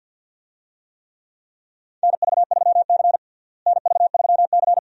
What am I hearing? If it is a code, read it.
Morse code, heard as a keyed tone, about 50 wpm, 700 Hz. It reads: NV3C NV3C
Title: USA Call Signs - 4 Characters - Repeated 2x - 50wpm